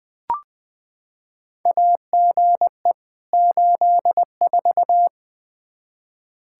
Morse code, heard as a keyed tone, about 21 wpm, 700 Hz. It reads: AGE 84
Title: K1USN SST QSO Practice 20wpm